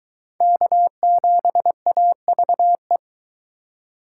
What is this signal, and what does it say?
Morse code, heard as a keyed tone, about 23 wpm, 700 Hz. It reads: K7AVE